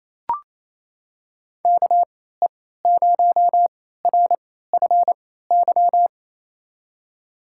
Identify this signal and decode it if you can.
Morse code, heard as a keyed tone, about 28 wpm, spaced as for 16 wpm, 700 Hz. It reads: KE0RFY
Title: USA Call Signs - 6 Characters - 3x Character Spacing - 28wpm